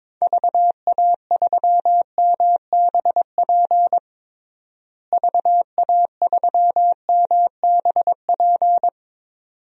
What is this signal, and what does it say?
Morse code, heard as a keyed tone, about 22 wpm, 700 Hz. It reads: VA3MBP VA3MBP
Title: DXCC-100 Call Signs - 6 Characters - Repeated 2x - 22wpm